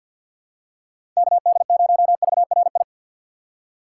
Morse code, heard as a keyed tone, about 52 wpm, 700 Hz. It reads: KD0VRI
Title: USA Call Signs - 6 Characters - 3x Character Spacing - 50wpm